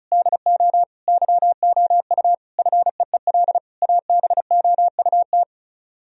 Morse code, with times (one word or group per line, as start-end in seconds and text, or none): 0.1-0.8 DO
1.1-2.3 YOU
2.6-3.6 FEEL
3.8-5.4 ABOUT